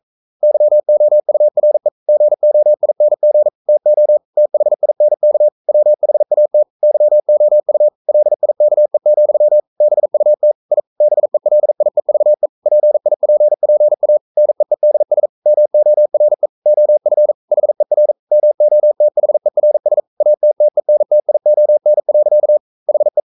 Morse code, heard as a keyed tone, about 42 wpm, 600 Hz. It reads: YOURE GOING TO THINK WHAT YOU LIKE, BUT I BELIEVE PIPPA NEEDS MORE OF HER MOTHERS ATTENTION. HE
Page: Romanen Never af Ken Follet på engelsk.